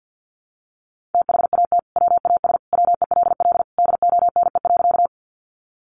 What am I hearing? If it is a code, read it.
Morse code, heard as a keyed tone, about 50 wpm, 700 Hz. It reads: THAT WAS WELL DONE.
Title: Sentences from Top 500 Words 50wpm